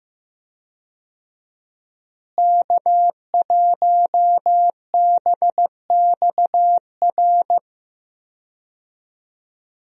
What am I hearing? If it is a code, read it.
Morse code, heard as a keyed tone, about 15 wpm, 700 Hz. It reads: K1BXR
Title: USA Call Signs - 5 Characters - Repeated 3x - 15wpm